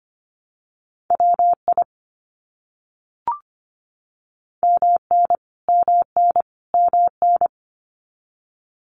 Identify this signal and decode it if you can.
Morse code, heard as a keyed tone, about 25 wpm, 700 Hz. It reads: WI MN MN MN
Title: US States Repeated 3x 25wpm